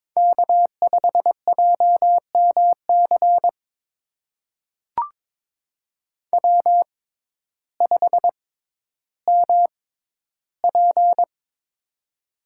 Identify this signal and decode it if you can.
Morse code, heard as a keyed tone, about 22 wpm, spaced as for 12 wpm, 700 Hz. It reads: K5JMC W5MP
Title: USA Call Signs - 6x Character Spacing - 22wpm